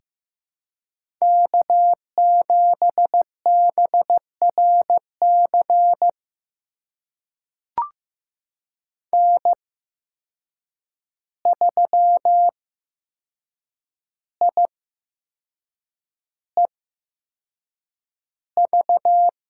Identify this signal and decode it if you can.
Morse code, heard as a keyed tone, about 15 wpm, spaced as for 6 wpm, 700 Hz. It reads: K7BRC N3IEV
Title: USA Call Signs - 8x Character Spacing - 15wpm